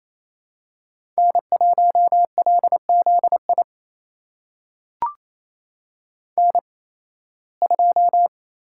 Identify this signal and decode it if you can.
Morse code, heard as a keyed tone, about 28 wpm, spaced as for 18 wpm, 700 Hz. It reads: N1LZI N2
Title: USA Call Signs - 5 Characters - 8x Character Spacing - 28wpm